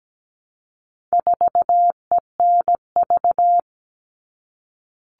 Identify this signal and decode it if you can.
Morse code, heard as a keyed tone, about 17 wpm, 700 Hz. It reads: 4ENV
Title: All Cut Numbers - 4-Digit 17wpm